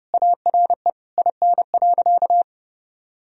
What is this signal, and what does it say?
Morse code, heard as a keyed tone, about 30 wpm, 700 Hz. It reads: ARE IN.
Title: Sentences from Top 1000 Words - Review Part 2 - 30wpm